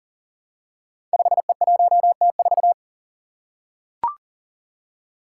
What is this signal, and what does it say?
Morse code, heard as a keyed tone, about 42 wpm, 700 Hz. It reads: 5E1T4
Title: Five Challenging Letter-Number Combinations 40wpm